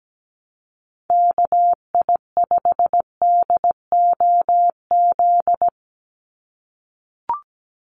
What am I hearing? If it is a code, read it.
Morse code, heard as a keyed tone, about 17 wpm, 700 Hz. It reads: KI5DOZ
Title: USA Call Signs 17wpm